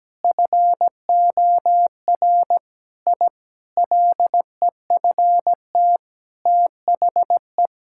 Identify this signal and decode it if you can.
Morse code, heard as a keyed tone, about 17 wpm, 700 Hz. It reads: FOR I LEFT THE
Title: Sentences from Top 1000 Words - Review Part 5 - 17wpm